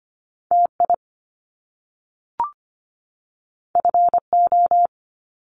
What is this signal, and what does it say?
Morse code, heard as a keyed tone, about 25 wpm, 700 Hz. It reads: TI FO
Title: Top Two Letter Combinations in Words 25wpm